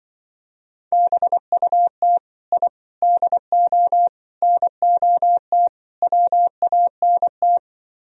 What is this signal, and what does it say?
Morse code, heard as a keyed tone, about 23 wpm, 700 Hz. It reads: BUT I DO NOT WANT
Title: Speed-Racing - Sets of 5 Words 20wpm